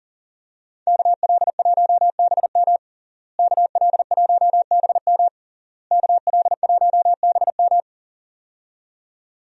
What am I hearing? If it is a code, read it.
Morse code, heard as a keyed tone, about 37 wpm, 700 Hz. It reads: KL1BM KL1BM KL1BM
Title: USA Call Signs - 5 Characters - Repeated 3x - 40wpm